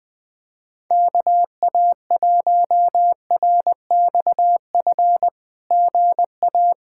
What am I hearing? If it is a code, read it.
Morse code, heard as a keyed tone, about 20 wpm, 700 Hz. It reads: KA1RXF GA
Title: SOTA QSO Practice 20wpm